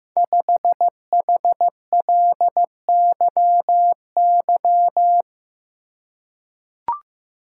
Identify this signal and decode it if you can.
Morse code, heard as a keyed tone, about 15 wpm, 700 Hz. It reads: 5HLYY
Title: Code Groups - 5 Letter:Numbers - 15wpm